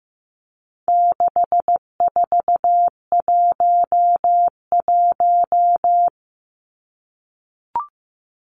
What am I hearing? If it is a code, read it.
Morse code, heard as a keyed tone, about 15 wpm, 700 Hz. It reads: 6411
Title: Numbers - 4-Digit 15wpm